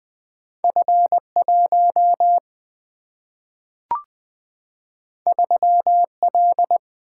Morse code, heard as a keyed tone, about 20 wpm, 700 Hz. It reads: F1 3L
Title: Single-Letter Single-Number 20wpm